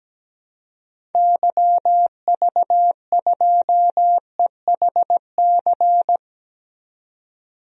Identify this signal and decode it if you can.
Morse code, heard as a keyed tone, about 17 wpm, 700 Hz. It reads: YV2EHC
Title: DXCC-100 Call Signs - 6 Characters - 17wpm